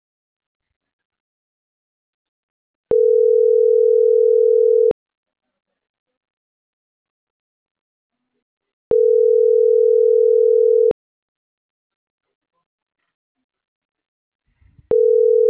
Ringback tone (North America), 440 + 480 Hz.